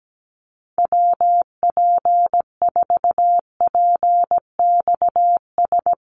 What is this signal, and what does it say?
Morse code, heard as a keyed tone, about 17 wpm, 700 Hz. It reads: WP4PXS
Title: USA Call Signs 17wpm